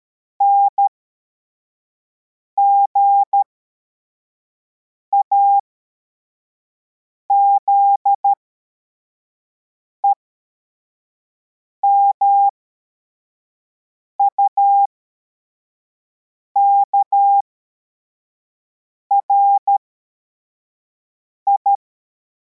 Morse code, heard as a keyed tone, about 13 wpm, spaced as for 4 wpm, 800 Hz. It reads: NGAZE MUKRI